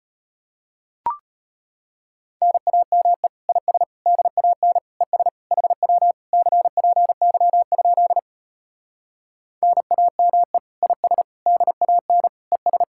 Morse code, heard as a keyed tone, about 33 wpm, 700 Hz. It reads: NAME IS DAN ES HW CPY? NAME IS DAN ES